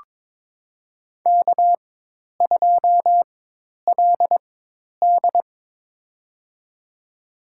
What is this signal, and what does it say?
Morse code, heard as a keyed tone, about 23 wpm, spaced as for 10 wpm, 700 Hz. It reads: K2LD
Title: USA Call Signs - 4 Characters - 4x Character Spacing - 22wpm